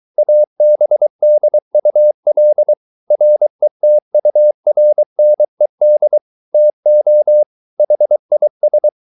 Morse code, heard as a keyed tone, about 23 wpm, 600 Hz. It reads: ABDUL RETURNED TO HIS